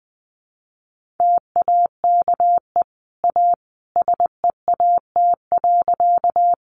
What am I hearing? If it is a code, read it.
Morse code, heard as a keyed tone, about 20 wpm, 700 Hz. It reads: TAKE A SEAT.
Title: Sentences from Top 1500 Words 20wpm